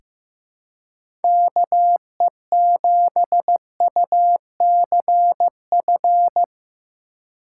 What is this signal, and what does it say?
Morse code, heard as a keyed tone, about 15 wpm, 700 Hz. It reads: KE7UCF
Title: USA Call Signs - 2x Character Spacing - 15wpm